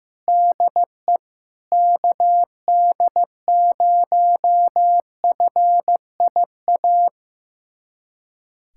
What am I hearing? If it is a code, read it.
Morse code, heard as a keyed tone, about 15 wpm, 700 Hz. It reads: DE KD0FIA
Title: USA Callsign DE Callsign 15wpm